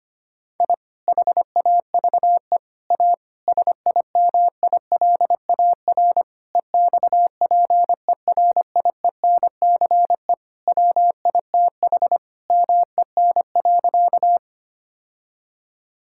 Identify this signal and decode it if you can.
Morse code, heard as a keyed tone, about 25 wpm, 700 Hz. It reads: I HAVE A SIMILAR EXPERIENCE WITH MEN.
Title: Sentences from Top 1000 Words - Review Part 5 - 25wpm